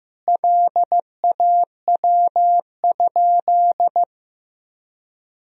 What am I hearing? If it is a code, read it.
Morse code, heard as a keyed tone, about 15 wpm, 700 Hz. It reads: LAW?
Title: Sentences from Top 1000 Words - Review Part 4 - 15wpm